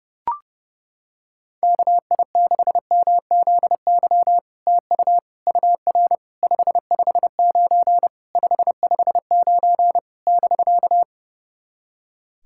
Morse code, heard as a keyed tone, about 30 wpm, 700 Hz. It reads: KI6MZY TU UR 559 559 <BK>
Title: SOTA QSO Practice 30wpm